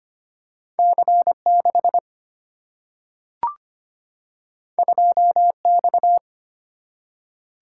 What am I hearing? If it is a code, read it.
Morse code, heard as a keyed tone, about 25 wpm, 700 Hz. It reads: C 6 2 X
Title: Single-Letter Single-Number 25wpm